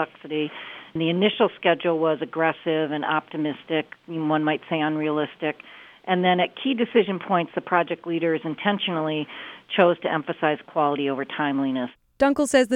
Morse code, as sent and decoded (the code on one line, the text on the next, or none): none